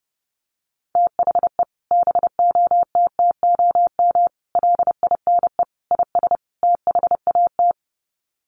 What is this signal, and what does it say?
Morse code, heard as a keyed tone, about 30 wpm, 700 Hz. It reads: THE BOTTOM LINE IS THAT
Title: Sets of 5 Words Encore 30wpm